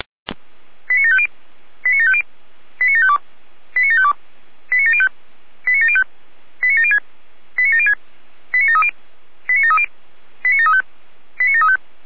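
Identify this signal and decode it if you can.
Five-tone selective call (ZVEI), heard as quick runs of five tones, tones of about 70 ms. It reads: 89750 89750 89752 89752 89705 89705 89706 89706 89730 89730 89735 89735
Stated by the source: ZVEI five-tone selective call received on an SDR.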